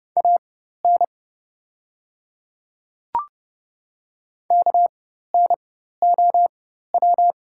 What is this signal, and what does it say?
Morse code, heard as a keyed tone, about 30 wpm, spaced as for 14 wpm, 690 Hz. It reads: AN KNOW
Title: 4x Character Spacing - Top 100 Words - 30wpm